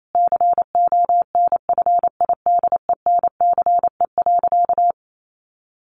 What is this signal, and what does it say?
Morse code, heard as a keyed tone, about 28 wpm, 700 Hz. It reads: CONFIDENCE.
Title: Sentences from Top 1500 Words 28wpm